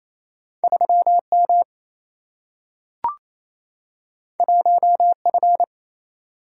Morse code, heard as a keyed tone, about 28 wpm, 700 Hz. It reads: 3M 1F
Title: Single-Letter Single-Number 28wpm